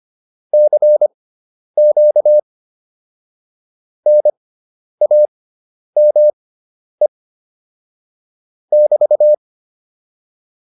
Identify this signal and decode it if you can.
Morse code, heard as a keyed tone, about 25 wpm, spaced as for 10 wpm, 600 Hz. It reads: CQ NAME <BT>
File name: QSO108 10